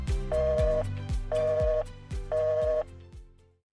Busy tone (North America), 480 + 620 Hz.
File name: Iransample / telephone / telephone-demo